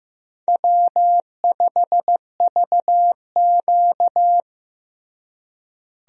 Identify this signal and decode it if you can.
Morse code, heard as a keyed tone, about 15 wpm, 700 Hz. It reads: W5VQ
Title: USA Call Signs - 4 Characters - Repeated 3x - 15wpm